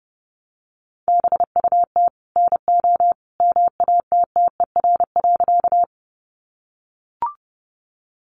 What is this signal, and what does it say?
Morse code, heard as a keyed tone, about 30 wpm, 700 Hz. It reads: BUT NO MATTER.
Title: Sentences from Top 700 Words 30wpm